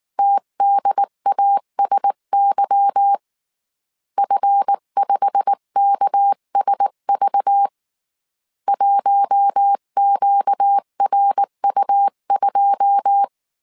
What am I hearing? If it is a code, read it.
Morse code, heard as a keyed tone, about 19 wpm, 800 Hz. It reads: TDASY F5KSV 1QRU2
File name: CW / DET / 1 9 1500C S B 18wpm 2551